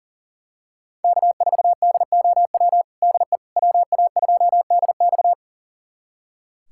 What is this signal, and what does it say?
Morse code, heard as a keyed tone, about 40 wpm, 700 Hz. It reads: K4DOW DE WA2DX
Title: USA Callsign DE Callsign - 5 Characters - 40wpm